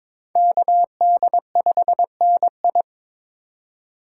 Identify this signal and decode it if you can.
Morse code, heard as a keyed tone, about 26 wpm, 700 Hz. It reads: KD5NI
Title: Speed Racing - USA 2x2 Call Signs 22wpm